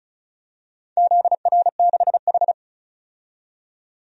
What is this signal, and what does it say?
Morse code, heard as a keyed tone, about 35 wpm, 700 Hz. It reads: ZR6H
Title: DXCC-100 Call Signs - 4 Characters - 35wpm